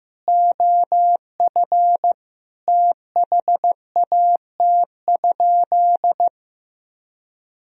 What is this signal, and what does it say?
Morse code, heard as a keyed tone, about 15 wpm, 700 Hz. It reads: OF THAT?
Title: Sentences from Top 1000 Words - Review Part 2 - 15wpm